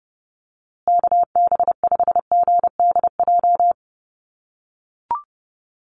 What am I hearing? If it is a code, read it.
Morse code, heard as a keyed tone, about 30 wpm, 700 Hz. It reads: KB5GDJ
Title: USA Call Signs Repeated 2x 30wpm